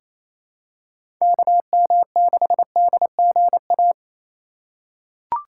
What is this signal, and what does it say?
Morse code, heard as a keyed tone, about 28 wpm, 690 Hz. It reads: KM6DGA
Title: USA Call Signs - 6 Characters - 2x Character Spacing - 28wpm